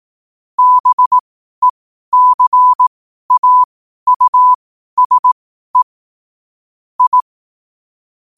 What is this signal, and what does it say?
Morse code, heard as a keyed tone, about 18 wpm, spaced as for 12 wpm, 1000 Hz. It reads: BECAUSE I
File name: qotdfiles / 12wpm